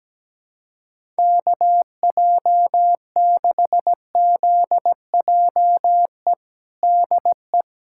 Morse code, heard as a keyed tone, about 17 wpm, 700 Hz. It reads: KJ6ZJE DE
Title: USA Callsign DE Callsign 17wpm